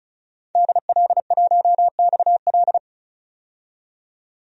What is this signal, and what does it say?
Morse code, heard as a keyed tone, about 35 wpm, 700 Hz. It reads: DL1XL
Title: DXCC-100 Call Signs - 5 Characters - 35wpm